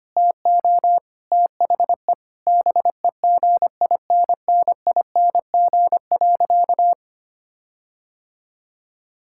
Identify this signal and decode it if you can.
Morse code, heard as a keyed tone, about 25 wpm, 700 Hz. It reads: TO THE BEGINNING.
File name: practice / 25wpm / Sentences with 4-Word Phrases - 25wpm